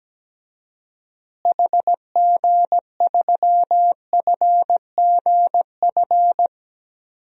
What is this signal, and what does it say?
Morse code, heard as a keyed tone, about 17 wpm, 700 Hz. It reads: HG3FGF
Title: DXCC-100 Call Signs - 6 Characters - Repeated 3x - 17wpm